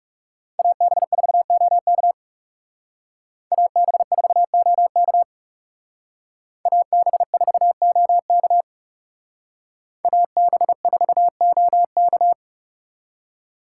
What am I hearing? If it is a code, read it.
Morse code, heard as a keyed tone, about 36 wpm, 700 Hz. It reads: AB4OK AB4OK AB4OK AB4OK
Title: Speed Racing - USA 2x2 Call Signs 30wpm